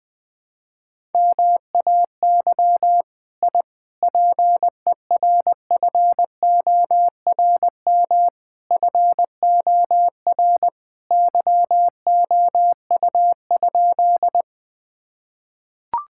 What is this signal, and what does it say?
Morse code, heard as a keyed tone, about 20 wpm, 700 Hz. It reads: MAY I PERFORM FOR YOU?
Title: Sentences from Top 1000 Words - Review Part 5 - 20wpm